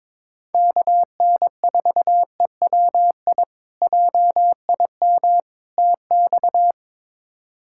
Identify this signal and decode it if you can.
Morse code, heard as a keyed tone, about 22 wpm, 700 Hz. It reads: KN4EWI JIM TX
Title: K1USN SST QSO Practice - 22wpm